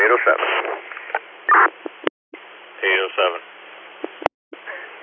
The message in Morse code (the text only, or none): none